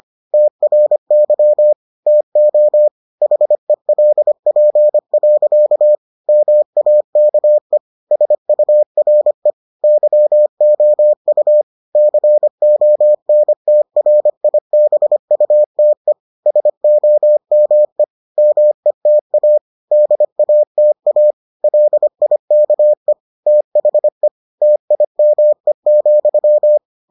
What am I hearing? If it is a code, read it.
Morse code, heard as a keyed tone, about 25 wpm, 600 Hz. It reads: TRY TO HELP. MAKE SURE YOU CONTRIBUTE SOME META DATA LIKE THE TIME,